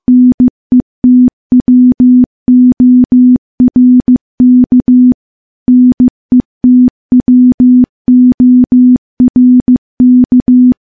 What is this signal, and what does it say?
Morse code, heard as a keyed tone, about 14 wpm, 250 Hz. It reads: NETWORK NETWORK
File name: morse / morse-training-15WPM